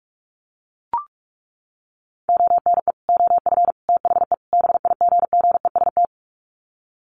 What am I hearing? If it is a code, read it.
Morse code, heard as a keyed tone, about 44 wpm, 700 Hz. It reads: ONE OF THE BIGGEST
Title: Sets of 4 Words 45wpm